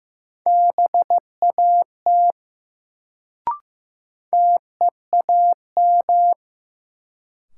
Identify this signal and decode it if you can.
Morse code, heard as a keyed tone, about 15 wpm, 700 Hz. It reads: BAT TEAM
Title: Sentences from Top 1500 Words - New Word Drill Alt - 15wpm